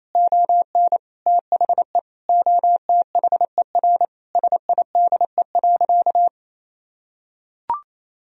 Morse code, heard as a keyed tone, about 28 wpm, 700 Hz. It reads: ON THE OTHER SIDE.
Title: Sentences from Top 1000 Words - Review Part 1 - 28wpm